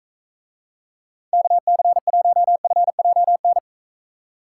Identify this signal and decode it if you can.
Morse code, heard as a keyed tone, about 42 wpm, 700 Hz. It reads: KC1FJN